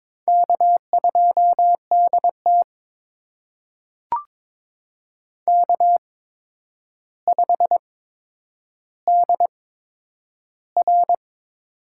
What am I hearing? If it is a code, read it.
Morse code, heard as a keyed tone, about 22 wpm, spaced as for 9 wpm, 700 Hz. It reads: K2DT K5DR
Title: USA Call Signs - 4 Characters - 8x Character Spacing - 22wpm